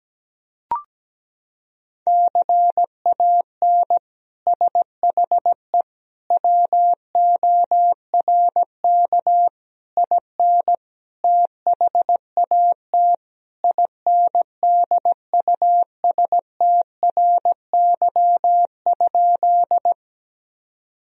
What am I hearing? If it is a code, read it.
Morse code, heard as a keyed tone, about 17 wpm, 700 Hz. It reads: CAN SHE WORK IN THAT INDUSTRY?
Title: Sentences from Top 1000 Words - Review Part 4 - 17wpm